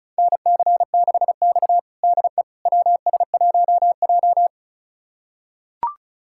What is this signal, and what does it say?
Morse code, heard as a keyed tone, about 36 wpm, 700 Hz. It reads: NC6X DE WS1J
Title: USA Callsign DE Callsign - 4 Characters - 35wpm